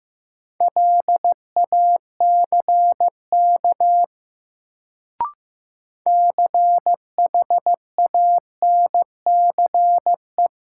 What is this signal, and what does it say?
Morse code, heard as a keyed tone, about 15 wpm, 700 Hz. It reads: LACK CHANCE
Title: Sets of 3 Words - New Word Drill - 15wpm